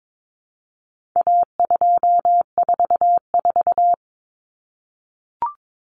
Morse code, heard as a keyed tone, about 22 wpm, 700 Hz. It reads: A244
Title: Common Cut Numbers - 4-Digit 22wpm